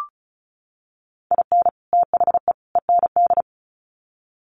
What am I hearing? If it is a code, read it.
Morse code, heard as a keyed tone, about 34 wpm, 700 Hz. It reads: IN THE END
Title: Sets of 3 Words 35wpm